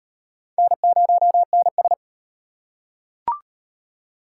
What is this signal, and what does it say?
Morse code, heard as a keyed tone, about 36 wpm, 700 Hz. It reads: N0NS